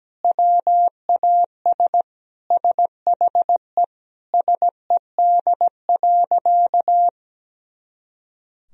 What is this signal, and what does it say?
Morse code, heard as a keyed tone, about 17 wpm, 700 Hz. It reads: WAS SHE SED.
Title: Sentences from Top 400 Words and Abbreviations 17wpm